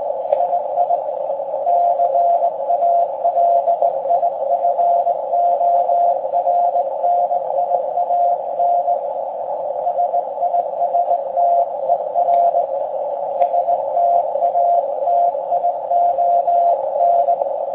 Morse code, heard as a keyed tone, about 19 wpm, 690 Hz. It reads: II CALIFORNIAN SISTER STATION